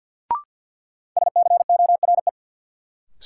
Morse code, heard as a keyed tone, about 52 wpm, 700 Hz. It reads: SCORE
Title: Sentences from Top 1600 Words - New Word Drill Alt - 50wpm